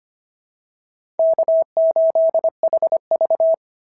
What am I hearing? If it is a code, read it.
Morse code, heard as a keyed tone, about 25 wpm, 690 Hz. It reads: K8HV
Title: SOTA QSO Practice 25wpm